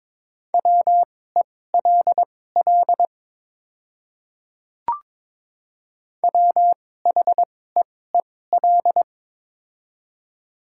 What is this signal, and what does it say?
Morse code, heard as a keyed tone, about 22 wpm, spaced as for 16 wpm, 700 Hz. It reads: WELL WHEEL
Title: 2x Character Spacing - Top 500 Words - 22wpm